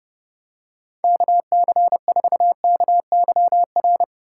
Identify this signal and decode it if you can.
Morse code, heard as a keyed tone, about 29 wpm, 700 Hz. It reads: KC4KYR